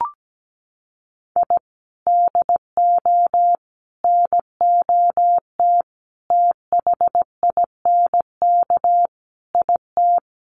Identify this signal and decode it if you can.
Morse code, heard as a keyed tone, about 17 wpm, 700 Hz. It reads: I DO NOT THINK IT
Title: Sets of 5 Words 17wpm